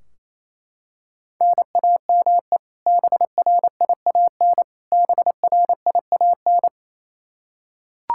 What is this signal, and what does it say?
Morse code, heard as a keyed tone, about 28 wpm, 700 Hz. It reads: NAME BRIAN BRIAN
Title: K1USN SST QSO Practice 28wpm